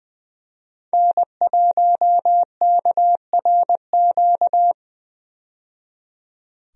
Morse code, heard as a keyed tone, about 20 wpm, 700 Hz. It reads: N1KRQ
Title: USA Call Signs - 5 Characters - 20wpm